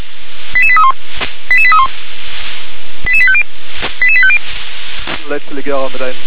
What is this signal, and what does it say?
Five-tone selective call (ZVEI), heard as quick runs of five tones, tones of about 70 ms, with speech. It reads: 88831 88831 88850 88850